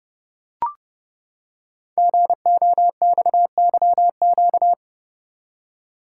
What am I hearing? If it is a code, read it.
Morse code, heard as a keyed tone, about 30 wpm, 700 Hz. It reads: GOXYQ
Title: Code Groups - 5 Letter:Numbers - 30wpm